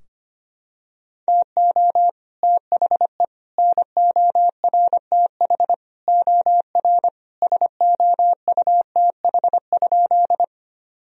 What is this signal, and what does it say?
Morse code, heard as a keyed tone, about 25 wpm, 700 Hz. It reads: TO THE NORTH OR SOUTH?
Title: Sentences from Top 1000 Words - Review Part 2 - 25wpm